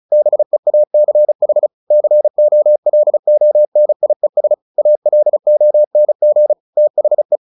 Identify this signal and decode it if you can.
Morse code, heard as a keyed tone, about 34 wpm, 600 Hz. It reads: BEACH COLONIES ALONG THE
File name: the old man and the sea 2 35wpm